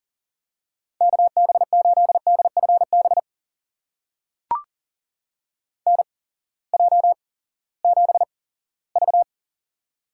Morse code, heard as a keyed tone, about 40 wpm, spaced as for 20 wpm, 700 Hz. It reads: KB8DFB NJ7V
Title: USA Call Signs - 8x Character Spacing - 40wpm